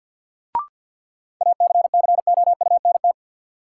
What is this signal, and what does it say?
Morse code, heard as a keyed tone, about 52 wpm, 700 Hz. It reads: ACCOUNT
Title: Sentences from Top 900 Words - New Word Drill - 50wpm